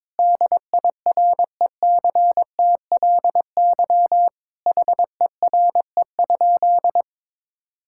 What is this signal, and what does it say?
Morse code, heard as a keyed tone, about 22 wpm, 700 Hz. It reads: DIRECTLY HERE?
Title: Sentences from Top 1000 Words - Review Part 5 - 22wpm